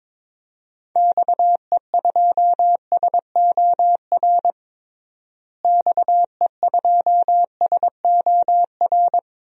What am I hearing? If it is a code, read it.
Morse code, heard as a keyed tone, about 22 wpm, 700 Hz. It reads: XE2SOR XE2SOR
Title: DXCC-100 Call Signs - 6 Characters - Repeated 2x - 22wpm